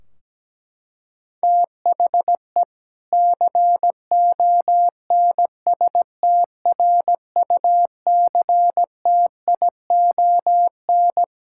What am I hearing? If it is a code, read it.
Morse code, heard as a keyed tone, about 17 wpm, 700 Hz. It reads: THE CONSTRUCTION
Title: Sentences from Top 1600 Words 17wpm